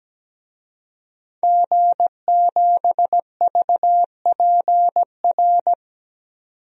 Morse code, heard as a keyed tone, about 17 wpm, 700 Hz. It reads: G7VPR
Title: DXCC-100 Call Signs - 5 Characters - 17wpm